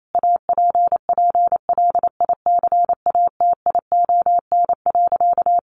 Morse code, heard as a keyed tone, about 28 wpm, 700 Hz. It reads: APPLICATION.
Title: Sentences from Top 1200 Words 28wpm